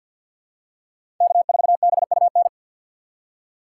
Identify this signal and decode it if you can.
Morse code, heard as a keyed tone, about 52 wpm, 700 Hz. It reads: K4BUN
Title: USA Call Signs - 5 Characters - 4x Character Spacing - 50wpm